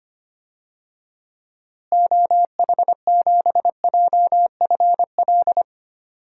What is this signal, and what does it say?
Morse code, heard as a keyed tone, about 25 wpm, 700 Hz. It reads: OH7JFL
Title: DXCC-100 Call Signs - 6 Characters - Repeated 3x - 25wpm